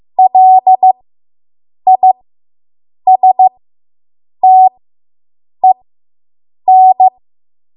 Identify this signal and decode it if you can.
Morse code, heard as a keyed tone, about 15 wpm, spaced as for 7 wpm, 750 Hz. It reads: LISTEN